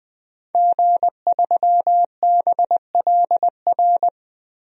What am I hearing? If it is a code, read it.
Morse code, heard as a keyed tone, about 20 wpm, 700 Hz. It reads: G3BLR
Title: DXCC-100 Call Signs - 5 Characters - Repeated 3x - 20wpm